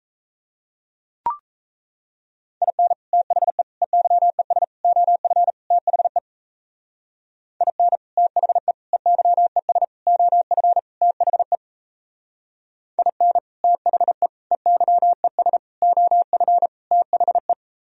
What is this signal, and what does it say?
Morse code, heard as a keyed tone, about 36 wpm, 700 Hz. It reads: IN THE EYES OF THE IN THE EYES OF THE IN THE EYES OF THE